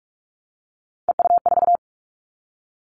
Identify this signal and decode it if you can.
Morse code, heard as a keyed tone, about 46 wpm, 700 Hz. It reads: EU4
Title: Callsign Prefixes for Top 50 DXCC - 45wpm